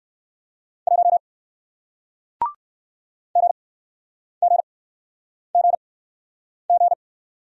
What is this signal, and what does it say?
Morse code, heard as a keyed tone, about 60 wpm, spaced as for 9 wpm, 700 Hz. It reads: 1 GGGG